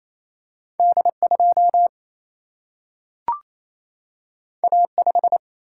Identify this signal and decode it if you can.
Morse code, heard as a keyed tone, about 28 wpm, 700 Hz. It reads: D 2 A 5
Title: Single-Letter Single-Number 28wpm